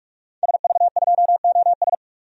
Morse code, heard as a keyed tone, about 41 wpm, 700 Hz. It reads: SV2OS
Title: DXCC-100 Call Signs - 5 Characters - 45wpm